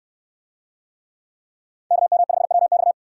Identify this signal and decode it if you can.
Morse code, heard as a keyed tone, about 66 wpm, 700 Hz. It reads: KM5WX